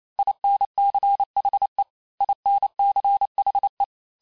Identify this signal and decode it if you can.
Morse code, heard as a keyed tone, about 28 wpm, 800 Hz. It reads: INCHE INCHE
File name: cours-du-mercredi / audios / cours-865-teste-les-mots-vitesse-27mpm